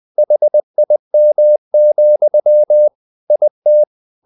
Morse code, heard as a keyed tone, about 20 wpm, 600 Hz. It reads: HIM, IT
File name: the old man and the sea part 2 20wpm